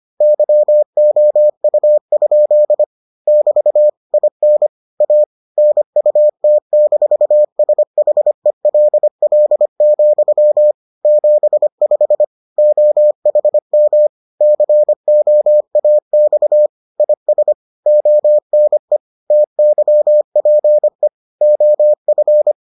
Morse code, heard as a keyed tone, about 25 wpm, 600 Hz. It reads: YOU? <BT> IN A NUT-SHELL, 75 OHM COAX IS ONE TYPE OF